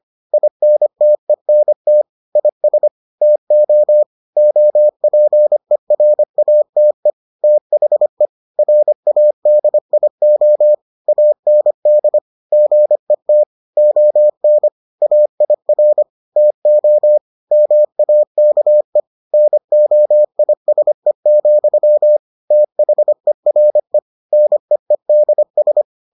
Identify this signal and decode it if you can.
Morse code, heard as a keyed tone, about 25 wpm, 600 Hz. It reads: INTENT IS TO OPERATE THE RADIO AND GET ON AIR TO MAKE NOISE, THERE NEEDS